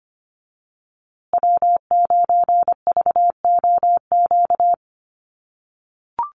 Morse code, heard as a keyed tone, about 25 wpm, 700 Hz. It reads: W9VOQ
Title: USA Call Signs Repeated 3x 25wpm